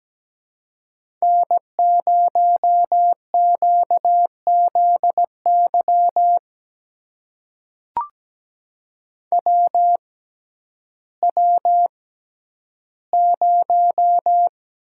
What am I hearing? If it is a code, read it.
Morse code, heard as a keyed tone, about 17 wpm, spaced as for 10 wpm, 700 Hz. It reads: N0QZY WW0
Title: USA Call Signs - 6x Character Spacing - 17wpm